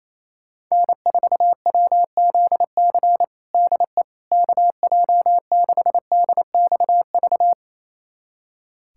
Morse code, heard as a keyed tone, about 28 wpm, 700 Hz. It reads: N4WZC DE KJ6DXV
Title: USA Callsign DE Callsign 28wpm